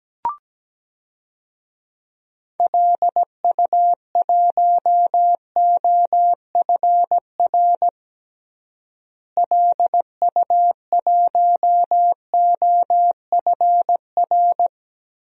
Morse code, heard as a keyed tone, about 17 wpm, 700 Hz. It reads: LU1OFR LU1OFR
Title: DXCC-100 Call Signs - 6 Characters - Repeated 3x - 17wpm